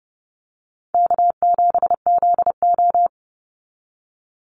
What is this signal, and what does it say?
Morse code, heard as a keyed tone, about 30 wpm, 700 Hz. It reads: K7ZO
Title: NA SOTA Chaser Callsigns 30wpm